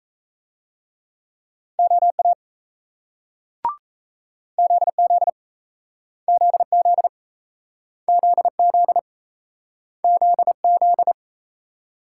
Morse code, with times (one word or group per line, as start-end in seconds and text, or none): 1.8-2.3 OA
4.6-5.3 ZZ
6.3-7.1 ZZ
8.1-9.0 ZZ
10.0-11.1 ZZ